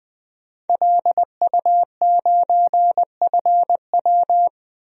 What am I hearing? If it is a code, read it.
Morse code, heard as a keyed tone, about 20 wpm, 700 Hz. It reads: LU9FW
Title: DXCC-100 Call Signs - 5 Characters - 20wpm